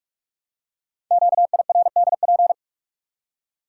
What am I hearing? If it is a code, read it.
Morse code, heard as a keyed tone, about 44 wpm, 700 Hz. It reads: QIRDP
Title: Code Groups - 5 Letter:Numbers - 45wpm